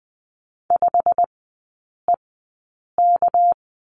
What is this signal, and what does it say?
Morse code, heard as a keyed tone, about 20 wpm, spaced as for 8 wpm, 700 Hz. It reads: 5EK
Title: 198 Lesson 32e - Letter k - Three Characters 20wpm